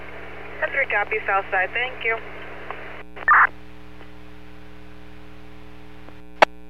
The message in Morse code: none

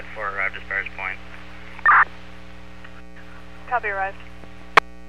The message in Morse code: none

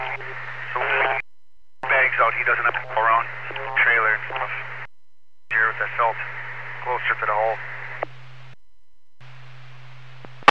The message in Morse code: none

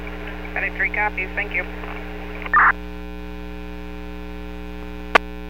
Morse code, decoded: none

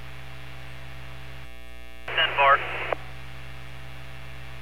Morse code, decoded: none

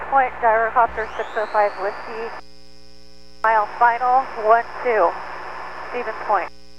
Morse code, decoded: none

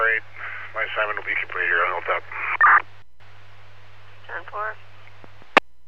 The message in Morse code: none

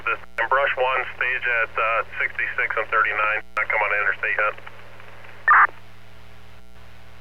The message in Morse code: none

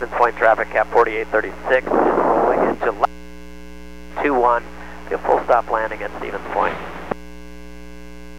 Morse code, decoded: none

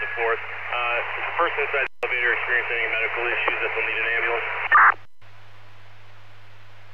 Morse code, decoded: none

